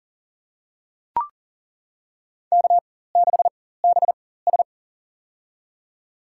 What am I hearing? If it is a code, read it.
Morse code, heard as a keyed tone, about 40 wpm, spaced as for 19 wpm, 700 Hz. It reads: K6BS
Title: USA Call Signs - 4 Characters - 4x Character Spacing - 40wpm